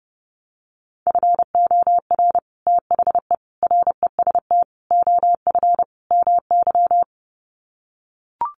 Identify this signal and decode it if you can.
Morse code, heard as a keyed tone, about 30 wpm, 700 Hz. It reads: FOR THE REST OF MY
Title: Sets of 5 Words 30wpm